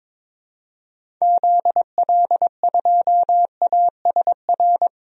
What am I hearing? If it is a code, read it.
Morse code, heard as a keyed tone, about 22 wpm, 700 Hz. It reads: ZL2ASR
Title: DXCC-100 Call Signs - 6 Characters - 22wpm